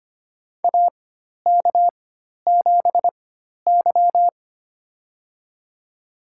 Morse code, heard as a keyed tone, about 25 wpm, spaced as for 12 wpm, 700 Hz. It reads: AK7Y
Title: USA Call Signs - 4 Characters - 4x Character Spacing - 25wpm